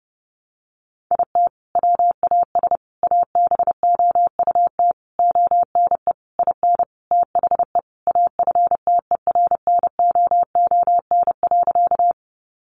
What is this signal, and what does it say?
Morse code, heard as a keyed tone, about 30 wpm, 700 Hz. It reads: IT WAS ABOUT ONE IN THE AFTERNOON.
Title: Sentences from Top 1500 Words 30wpm